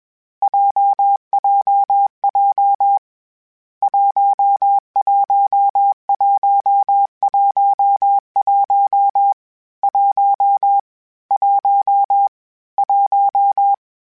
Morse code, heard as a keyed tone, about 21 wpm, 800 Hz. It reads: JJJ 11111 1 1 1
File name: Difficult Combos / Difficult Combos-mp3s / AWJ1 @20